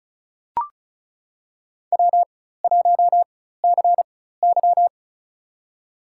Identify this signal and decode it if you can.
Morse code, heard as a keyed tone, about 34 wpm, spaced as for 16 wpm, 700 Hz. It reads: W1CY
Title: USA Call Signs - 4 Characters - 4x Character Spacing - 35wpm